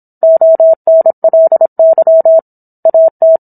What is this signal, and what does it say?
Morse code, heard as a keyed tone, about 27 wpm, 650 Hz. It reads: ONLY AT